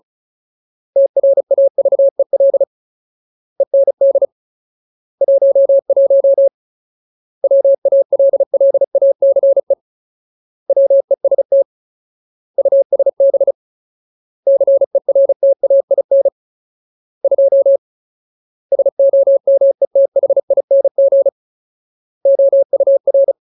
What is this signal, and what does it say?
Morse code, heard as a keyed tone, about 34 wpm, 550 Hz. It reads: TRAVEL END 11 WALLACE WEST USB CERTAIN 2 SOMETHING OUR
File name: Ultimate Word List / 35WPM Ultimate